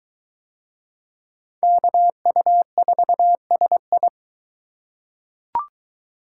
Morse code, heard as a keyed tone, about 22 wpm, 700 Hz. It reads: KU4SI